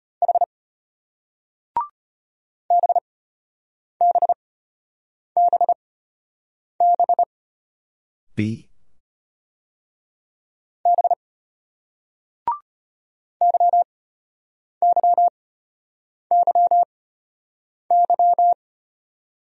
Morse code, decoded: H B B B B B Y Y Y Y

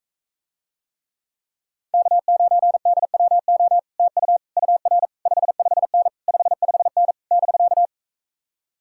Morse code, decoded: K9DWO TU UR 55N 55N <BK>

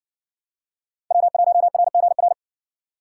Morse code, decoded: W1RGR

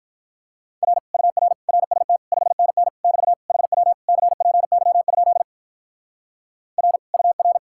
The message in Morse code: R UR RST 5NN <BT> HW CPY? R UR

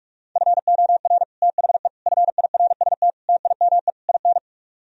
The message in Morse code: FOR THE FIRST TIME IN

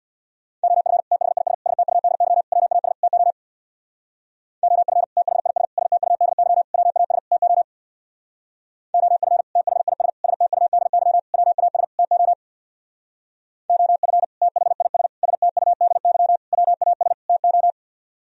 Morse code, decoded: OF THIS STUDY WAS TO OF THIS STUDY WAS TO OF THIS STUDY WAS TO OF THIS STUDY WAS TO

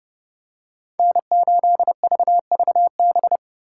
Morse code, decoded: N8VVB

